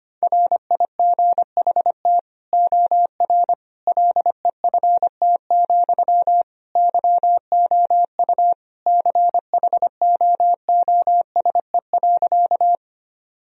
Morse code, heard as a keyed tone, about 25 wpm, 700 Hz. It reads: RIGHT OR LEFT, YOU CHOOSE.
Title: Sentences from Top 1000 Words - Review Part 4 - 25wpm